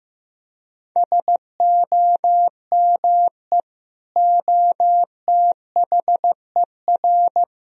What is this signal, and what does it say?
Morse code, heard as a keyed tone, about 16 wpm, 700 Hz. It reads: SOME OTHER
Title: Sentences from Top 1000 Words - Review Part 4 - 15wpm